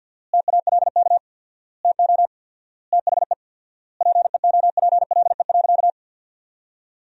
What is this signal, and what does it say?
Morse code, heard as a keyed tone, about 50 wpm, 700 Hz. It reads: TALK TO THE PEOPLE.
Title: 4x Word-Spacing - Sentences from Top 400 Words - 50wpm